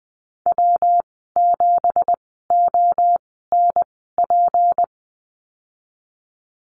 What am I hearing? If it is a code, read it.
Morse code, heard as a keyed tone, about 20 wpm, spaced as for 15 wpm, 700 Hz. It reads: W7ONP
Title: 179 Lesson 29g - Slash - Call Signs 20wpm